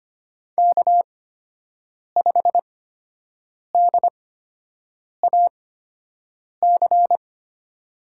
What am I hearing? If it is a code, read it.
Morse code, heard as a keyed tone, about 25 wpm, spaced as for 7 wpm, 700 Hz. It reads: K5DAC
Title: USA Call Signs - 5 Characters - 8x Character Spacing - 25wpm